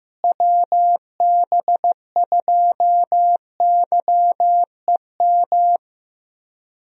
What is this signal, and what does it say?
Morse code, heard as a keyed tone, about 15 wpm, 700 Hz. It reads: WB2YEM
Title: USA Call Signs - 6 Characters - Repeated 3x - 15wpm